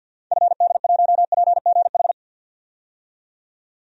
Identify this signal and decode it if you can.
Morse code, heard as a keyed tone, about 53 wpm, 700 Hz. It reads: FD1PGH
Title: DXCC-100 Call Signs - 6 Characters - 50wpm